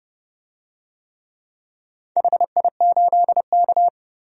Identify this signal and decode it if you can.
Morse code, heard as a keyed tone, about 29 wpm, 700 Hz. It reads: HI8K
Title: DXCC-100 Call Signs - 4 Characters - 30wpm